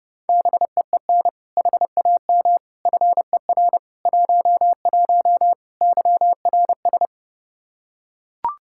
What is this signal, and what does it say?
Morse code, heard as a keyed tone, about 30 wpm, 700 Hz. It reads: BEEN HAM FER 11 YRS
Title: Practice QSO - Exchange 3 - More Info - 30wpm